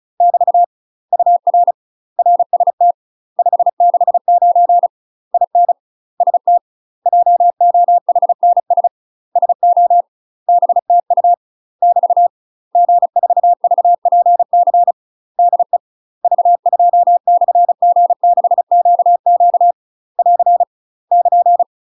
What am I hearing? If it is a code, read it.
Morse code, heard as a keyed tone, about 34 wpm, 700 Hz. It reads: <BT> UR RST 569 IN ST JOHNS SO BTU <BT> G4VPC DE V2/G6QQ <AR> <KN>
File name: UK MASTER QSOs TRAINING / 35WPM / UK MASTER QSOs 0021